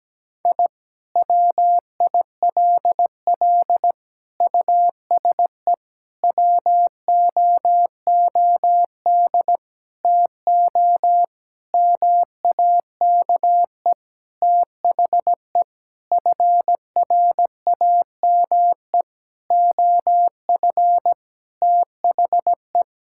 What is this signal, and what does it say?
Morse code, heard as a keyed tone, about 17 wpm, 700 Hz. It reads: I WILL USE WOOD TO MAKE THE FRAME OF THE
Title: Sentences from Top 1000 Words - Review Part 6 - 17wpm